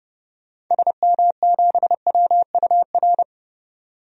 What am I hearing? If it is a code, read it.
Morse code, heard as a keyed tone, about 30 wpm, 700 Hz. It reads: SM7WUR
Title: DXCC-100 Call Signs - 6 Characters - 30wpm